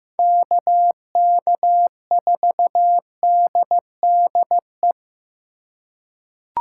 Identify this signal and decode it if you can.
Morse code, heard as a keyed tone, about 15 wpm, 700 Hz. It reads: KK4DDE
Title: USA Call Signs - 6 Characters - 2x Character Spacing - 15wpm